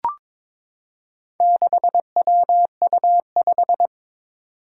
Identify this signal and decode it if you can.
Morse code, heard as a keyed tone, about 22 wpm, 700 Hz. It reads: 6WU5
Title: Four Challenging Letter-Number Combinations 22wpm